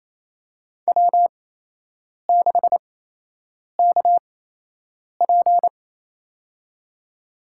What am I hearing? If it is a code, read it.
Morse code, heard as a keyed tone, about 28 wpm, spaced as for 8 wpm, 700 Hz. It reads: W6KP
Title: USA Call Signs - 4 Characters - 8x Character Spacing - 28wpm